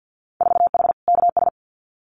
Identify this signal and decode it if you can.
Morse code, heard as a keyed tone, about 50 wpm, 700 Hz. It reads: 4H KS
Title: Winter Field Day Exchanges Encore 50wpm